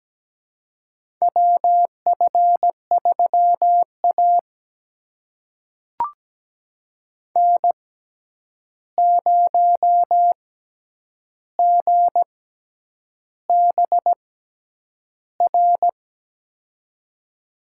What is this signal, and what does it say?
Morse code, heard as a keyed tone, about 17 wpm, spaced as for 8 wpm, 700 Hz. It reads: WF3A N0GBR
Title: USA Call Signs - 6x Character Spacing - 17wpm